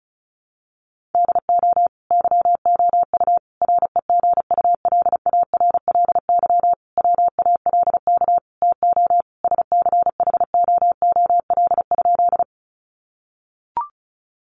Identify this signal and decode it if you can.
Morse code, heard as a keyed tone, about 34 wpm, 700 Hz. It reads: DO YOU REGULARLY WALK TO SCHOOL?
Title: Sentences from Top 1400 Words 35wpm